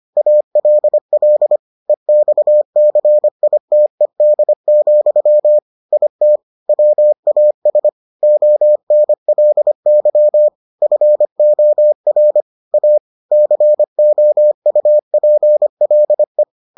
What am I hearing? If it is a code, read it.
Morse code, heard as a keyed tone, about 25 wpm, 600 Hz. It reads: ALL EXCITED, IT WAS ONLY FOR A COUPLE